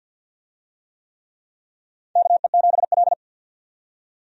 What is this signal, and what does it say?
Morse code, heard as a keyed tone, about 51 wpm, 700 Hz. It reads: KE7L
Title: USA Call Signs - 4 Characters - 50wpm